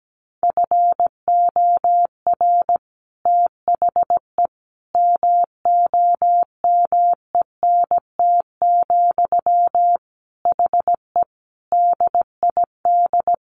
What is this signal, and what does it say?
Morse code, heard as a keyed tone, about 17 wpm, 700 Hz. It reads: FOR THE MOMENT, HE DID
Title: Sentences from Top 1400 Words 17wpm